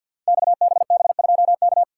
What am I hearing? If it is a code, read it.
Morse code, heard as a keyed tone, about 52 wpm, 700 Hz. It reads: XBB2K
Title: Code Groups - 5 Letter:Numbers - 50wpm